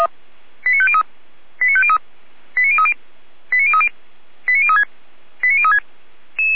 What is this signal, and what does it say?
Five-tone selective call (ZVEI), heard as quick runs of five tones, tones of about 70 ms. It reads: 89503 89503 89030 89030 89036 89036
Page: audio.f85360.feuerwehr-orlamuende.org